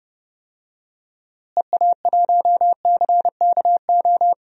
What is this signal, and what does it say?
Morse code, heard as a keyed tone, about 29 wpm, 700 Hz. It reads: EA1CKO
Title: DXCC-100 Call Signs - 6 Characters - Repeated 2x - 30wpm